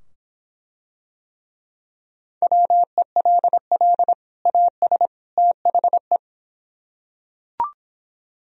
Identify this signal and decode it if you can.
Morse code, heard as a keyed tone, about 24 wpm, 700 Hz. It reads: WELL AS THE